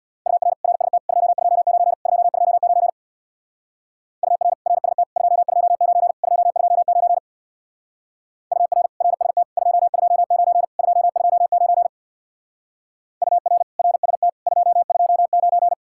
Morse code, heard as a keyed tone, about 64 wpm, 700 Hz. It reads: UR RST 229 229 UR RST 229 229 UR RST 229 229 UR RST 229